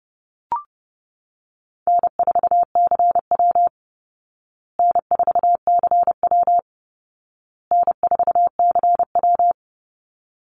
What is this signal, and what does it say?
Morse code, heard as a keyed tone, about 30 wpm, 700 Hz. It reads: N4CW N4CW N4CW
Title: Call Signs Repeated 2x - 30wpm